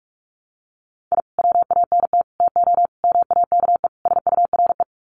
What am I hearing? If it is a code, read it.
Morse code, heard as a keyed tone, about 44 wpm, 700 Hz. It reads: I WANT TO MAKE SURE
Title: Sets of 5 Words Encore2 45wpm